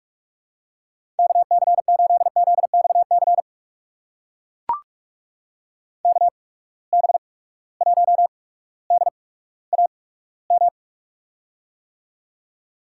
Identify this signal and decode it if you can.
Morse code, heard as a keyed tone, about 42 wpm, spaced as for 19 wpm, 700 Hz. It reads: KC8ZXC KB1DAM
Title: USA Call Signs - 6 Characters - 8x Character Spacing - 45wpm